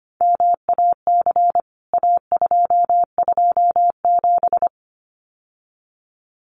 Morse code, heard as a keyed tone, about 25 wpm, 700 Hz. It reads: MAC A227
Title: CWT Exchange 25wpm